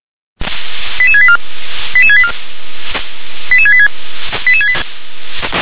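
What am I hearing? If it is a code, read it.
Five-tone selective call (ZVEI), heard as quick runs of five tones, tones of about 70 ms. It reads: 88684 88684 88686
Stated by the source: ZVEI five-tone selective call received by SDR.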